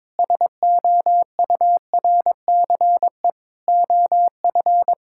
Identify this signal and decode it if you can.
Morse code, heard as a keyed tone, about 22 wpm, 700 Hz. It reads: SOURCE OF
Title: Sentences from Top 1000 Words - Review Part 5 - 22wpm